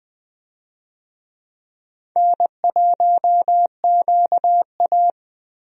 Morse code, heard as a keyed tone, about 21 wpm, 700 Hz. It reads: N1QA
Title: USA Call Signs - 4 Characters - Repeated 2x - 20wpm